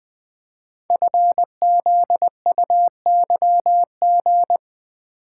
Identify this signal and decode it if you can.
Morse code, heard as a keyed tone, about 20 wpm, 700 Hz. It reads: FZUYG
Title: Code Groups - 5 Letter:Numbers - 20wpm